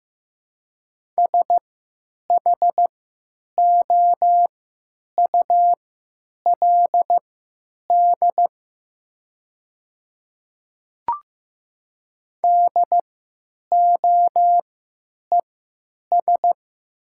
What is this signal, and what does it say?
Morse code, heard as a keyed tone, about 15 wpm, spaced as for 9 wpm, 700 Hz. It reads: SHOULD DOES
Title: 3x Character Spacing - Top 300 Words - 15wpm